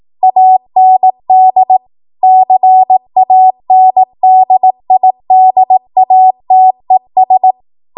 Morse code, heard as a keyed tone, about 18 wpm, 750 Hz. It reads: AND CANDIDATES